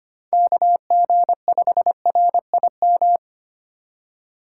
Morse code, heard as a keyed tone, about 25 wpm, 700 Hz. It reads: KG5RIM
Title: USA Call Signs - 6 Characters - 2x Character Spacing - 25wpm